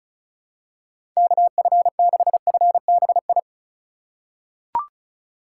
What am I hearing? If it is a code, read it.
Morse code, heard as a keyed tone, about 36 wpm, 700 Hz. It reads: KF6FBI
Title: USA Call Signs - 6 Characters - 2x Character Spacing - 35wpm